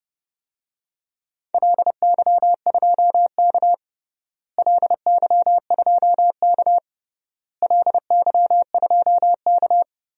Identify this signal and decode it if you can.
Morse code, heard as a keyed tone, about 30 wpm, 700 Hz. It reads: LY2K LY2K LY2K
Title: DXCC-100 Call Signs - 4 Characters - Repeated 3x - 30wpm